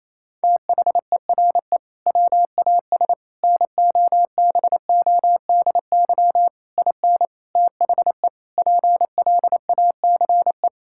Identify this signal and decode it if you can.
Morse code, heard as a keyed tone, about 28 wpm, 700 Hz. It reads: THERE WAS NOBODY IN THE PLACE
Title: Sentences from Top 1600 Words 28wpm